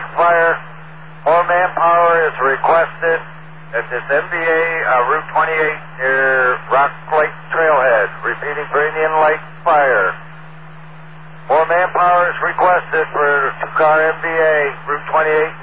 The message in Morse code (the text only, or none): none